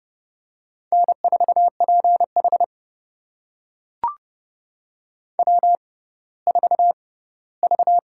Morse code, heard as a keyed tone, about 30 wpm, spaced as for 17 wpm, 700 Hz. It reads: N4PH W4V
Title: USA Call Signs - 4 Characters - 6x Character Spacing - 30wpm